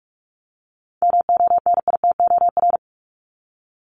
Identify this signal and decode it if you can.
Morse code, heard as a keyed tone, about 44 wpm, 700 Hz. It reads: MONITOR
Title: Top 1751-2000 Words 45wpm